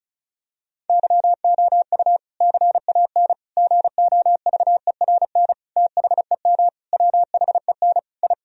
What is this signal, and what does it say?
Morse code, heard as a keyed tone, about 34 wpm, 700 Hz. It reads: YOU CAN GOVERN THEM WHEN I
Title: Sentences from Top 1000 Words - Review Part 2 - 35wpm